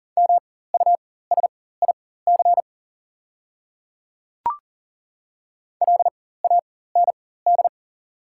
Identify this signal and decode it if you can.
Morse code, heard as a keyed tone, about 40 wpm, spaced as for 19 wpm, 700 Hz. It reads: MUSIC LAND
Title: 4x Character Spacing - Top 500 Words - 40wpm